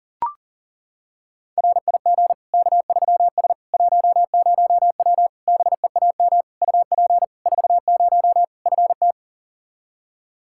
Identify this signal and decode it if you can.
Morse code, heard as a keyed tone, about 40 wpm, 700 Hz. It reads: RIG K3S 10W BEAM UP 40 FT
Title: Practice QSO - Exchange 2 - Info - 40wpm